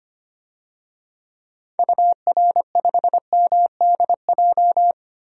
Morse code, heard as a keyed tone, about 25 wpm, 700 Hz. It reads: UR5MDJ
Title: DXCC-100 Call Signs - 6 Characters - Repeated 3x - 25wpm